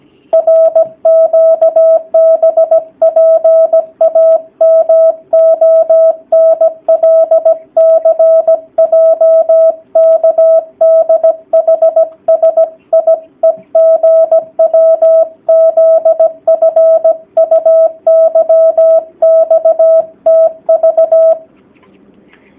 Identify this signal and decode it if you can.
Morse code, heard as a keyed tone, about 17 wpm, 650 Hz. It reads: RQBPAMONLCJKDHSIEGWZFUYXTV
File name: jh8cbh / 2019cw / sp16- w